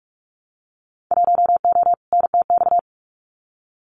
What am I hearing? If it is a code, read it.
Morse code, heard as a keyed tone, about 46 wpm, 700 Hz. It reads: 1O NTX
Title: Winter Field Day Exchanges 45wpm